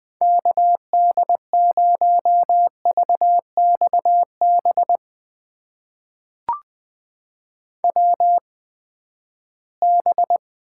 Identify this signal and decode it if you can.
Morse code, heard as a keyed tone, about 20 wpm, spaced as for 14 wpm, 700 Hz. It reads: KD0VXB WB
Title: USA Call Signs - 6 Characters - 8x Character Spacing - 20wpm